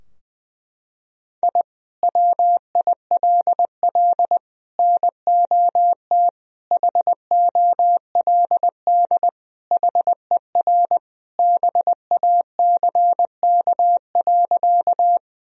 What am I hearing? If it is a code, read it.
Morse code, heard as a keyed tone, about 21 wpm, 700 Hz. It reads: I WILL NOT HOLD HER BACK.
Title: Sentences from Top 1000 Words - Review Part 2 - 20wpm